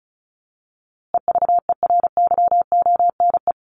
Morse code, heard as a keyed tone, about 34 wpm, 700 Hz. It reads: EVERYONE
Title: A Sound Of Thunder 35wpm